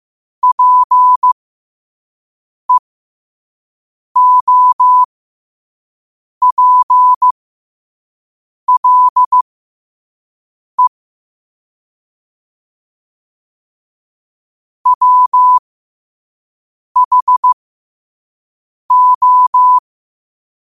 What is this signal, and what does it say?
Morse code, heard as a keyed tone, about 15 wpm, spaced as for 5 wpm, 1000 Hz. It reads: PEOPLE WHO
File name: qotdfiles / 5wpm